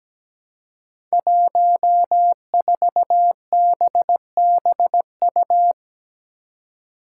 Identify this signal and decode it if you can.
Morse code, heard as a keyed tone, about 17 wpm, 700 Hz. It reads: 14BBU
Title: Five Challenging Letter-Number Combinations 17wpm